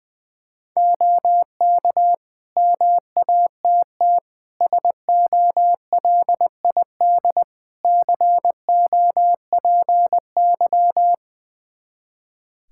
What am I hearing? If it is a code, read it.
Morse code, heard as a keyed tone, about 20 wpm, 700 Hz. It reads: OK MATT SOLID COPY
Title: Practice QSO - Exchange 3 - More Info - 20wpm